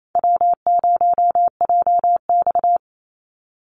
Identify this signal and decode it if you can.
Morse code, heard as a keyed tone, about 27 wpm, 700 Hz. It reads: W0JX
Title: Call Signs - 28wpm